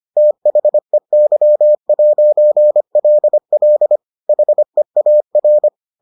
Morse code, heard as a keyed tone, about 25 wpm, 600 Hz. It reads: THEY'LL HEAR